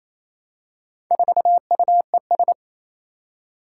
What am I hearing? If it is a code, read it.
Morse code, heard as a keyed tone, about 27 wpm, 700 Hz. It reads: 4UES
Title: Four Challenging Letter-Number Combinations 28wpm